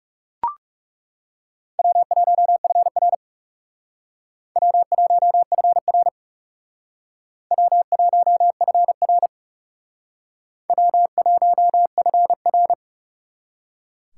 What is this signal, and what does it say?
Morse code, heard as a keyed tone, about 38 wpm, 700 Hz. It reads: W1FR W1FR W1FR W1FR